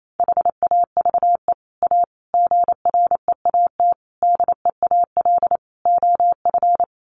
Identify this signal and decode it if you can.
Morse code, heard as a keyed tone, about 28 wpm, 700 Hz. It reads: HAVE A GREAT DEAL OF
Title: Sets of 5 Words Encore2 28wpm